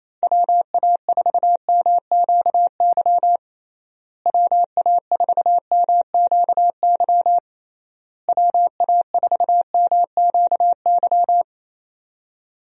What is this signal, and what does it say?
Morse code, heard as a keyed tone, about 27 wpm, 700 Hz. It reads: WA4MQY WA4MQY WA4MQY
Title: USA Call Signs - 6 Characters - Repeated 3x - 28wpm